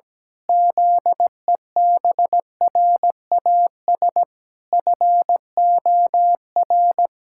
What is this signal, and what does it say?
Morse code, heard as a keyed tone, about 17 wpm, 700 Hz. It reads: ZEBRAS FOR
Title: Pangram Challenge v2 - 17wpm